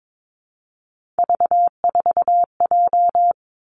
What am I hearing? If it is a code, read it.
Morse code, heard as a keyed tone, about 22 wpm, 700 Hz. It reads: V4J
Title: Three Challenging Letter-Number Combinations 22wpm